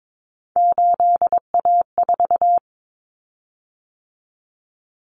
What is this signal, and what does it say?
Morse code, heard as a keyed tone, about 22 wpm, 700 Hz. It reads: 8A4
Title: Common Cut Numbers - 3-Digit 22wpm